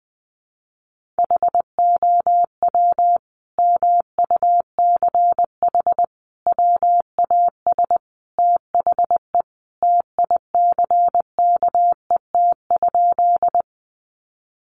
Morse code, heard as a keyed tone, about 20 wpm, 700 Hz. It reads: HOW MUCH WAS THE TICKET?
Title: Sentences from Top 1400 Words 20wpm